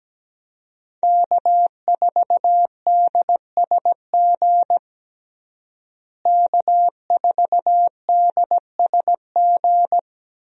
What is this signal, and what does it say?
Morse code, heard as a keyed tone, about 17 wpm, 700 Hz. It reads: K4DSG K4DSG
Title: USA Call Signs - 5 Characters - Repeated 2x - 17wpm